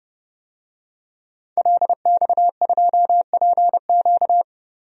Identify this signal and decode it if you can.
Morse code, heard as a keyed tone, about 30 wpm, 700 Hz. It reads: LX2PQ
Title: DXCC-100 Call Signs - 5 Characters - 30wpm